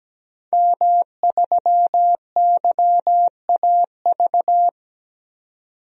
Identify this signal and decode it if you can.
Morse code, heard as a keyed tone, about 17 wpm, 700 Hz. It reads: M3YAV
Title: DXCC-100 Call Signs - 5 Characters - 17wpm